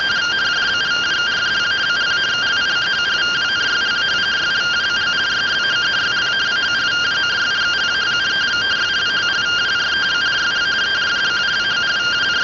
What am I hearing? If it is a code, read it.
Two-tone teleprinter signal, about 170 Hz shift.